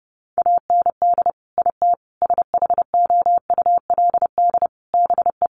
Morse code, heard as a keyed tone, about 30 wpm, 700 Hz. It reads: AND IT SHOULD BE
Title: Sentences from Top 1500 Words 30wpm